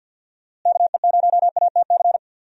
Morse code, heard as a keyed tone, about 52 wpm, 700 Hz. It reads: KE0ATC
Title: USA Call Signs - 6 Characters - Repeated 3x - 50wpm